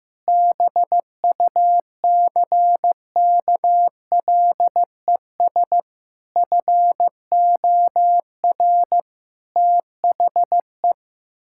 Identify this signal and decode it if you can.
Morse code, heard as a keyed tone, about 15 wpm, 700 Hz. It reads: BUCKLES FOR THE
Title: Pangram Challenge v2 - 15wpm